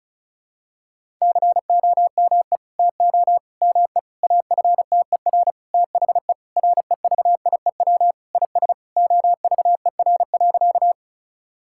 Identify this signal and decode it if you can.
Morse code, heard as a keyed tone, about 36 wpm, 700 Hz. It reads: COME TO ME AFTER THE REVIEW IS OVER.
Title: Sentences from Top 1000 Words - Review Part 6 - 35wpm